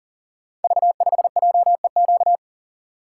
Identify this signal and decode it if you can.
Morse code, heard as a keyed tone, about 41 wpm, 690 Hz. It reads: V5JEQ
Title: Code Groups - 5 Letter:Numbers - 40wpm